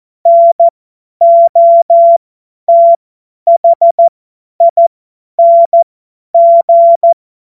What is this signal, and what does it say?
Morse code, heard as a keyed tone, about 14 wpm, spaced as for 10 wpm, 680 Hz. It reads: NOTHING
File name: Morsekurs / Witze / Witz 14. Dezember 2021, 50 F70